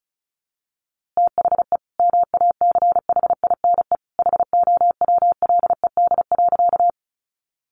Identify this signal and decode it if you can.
Morse code, heard as a keyed tone, about 34 wpm, 700 Hz. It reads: THE MACHINE HOWLED.
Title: A Sound Of Thunder 35wpm